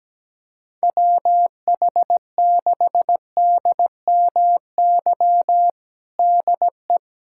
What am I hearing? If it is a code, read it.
Morse code, heard as a keyed tone, about 17 wpm, 700 Hz. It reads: WH6DMY DE
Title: USA Callsign DE Callsign - 6 Characters - 17wpm